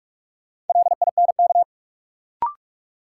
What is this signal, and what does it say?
Morse code, heard as a keyed tone, about 43 wpm, 700 Hz. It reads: LINK
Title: Sentences from Top 1000 Words - Review Part 7 - New Word Drill - 45wpm